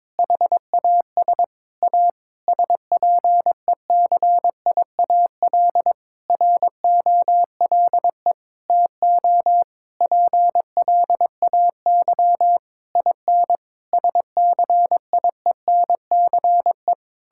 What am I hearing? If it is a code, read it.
Morse code, heard as a keyed tone, about 22 wpm, 700 Hz. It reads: HAS A SPECIAL ROLE TO PLAY IN SCIENCE
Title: Sentences from Top 1000 Words - Review Part 6 - 22wpm